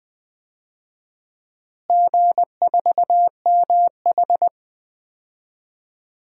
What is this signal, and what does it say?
Morse code, heard as a keyed tone, about 20 wpm, 700 Hz. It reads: G4MH
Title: DXCC-100 Call Signs - 4 Characters - Repeated 2x - 20wpm